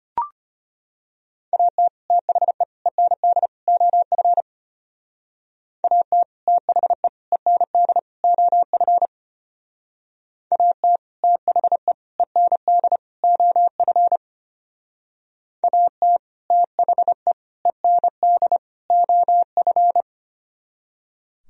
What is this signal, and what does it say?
Morse code, heard as a keyed tone, about 31 wpm, 700 Hz. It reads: AT THE END OF AT THE END OF AT THE END OF AT THE END OF
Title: Speed-Racing - Sets of 4 Words 25wpm